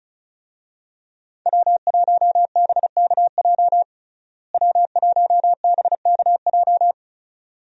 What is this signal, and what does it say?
Morse code, heard as a keyed tone, about 35 wpm, 700 Hz. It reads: W1BKJ W1BKJ
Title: USA Call Signs - 5 Characters - Repeated 2x - 35wpm